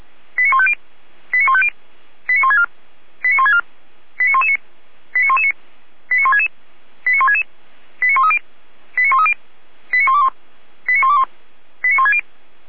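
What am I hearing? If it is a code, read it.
Five-tone selective call (ZVEI), heard as quick runs of five tones, tones of about 70 ms. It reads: 89150 89150 89164 89164 89119 89119 89166 89166 89130 89130 89121 89121 89160
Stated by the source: SDR-received ZVEI five-tone selective call, code 89072.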